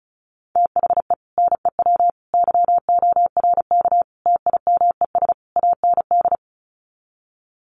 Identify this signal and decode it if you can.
Morse code, heard as a keyed tone, about 34 wpm, 700 Hz. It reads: THE NEW YORK TIMES AND
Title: Sets of 5 Words Encore 35wpm